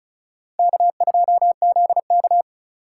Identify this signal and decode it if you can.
Morse code, heard as a keyed tone, about 34 wpm, 700 Hz. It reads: K2ZK
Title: USA Call Signs - 4 Characters - 4x Character Spacing - 35wpm